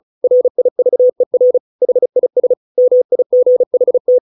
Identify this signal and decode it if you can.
Morse code, heard as a keyed tone, about 34 wpm, 500 Hz. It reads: RIVER HIS MIGHT